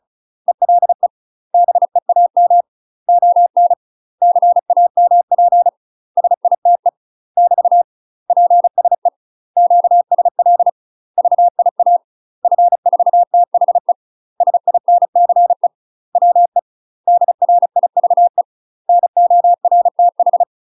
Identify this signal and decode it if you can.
Morse code, heard as a keyed tone, about 35 wpm, 700 Hz. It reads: ELE BEAM ON CAMP SITE <BT> PSE QSL VIA F4THE SINCE WE DRIVE NORTH
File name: UK MASTER QSOs TRAINING/35WPM